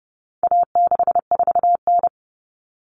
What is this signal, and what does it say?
Morse code, heard as a keyed tone, about 30 wpm, 700 Hz. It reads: A64N
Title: Common Cut Numbers - 4-Digit 30wpm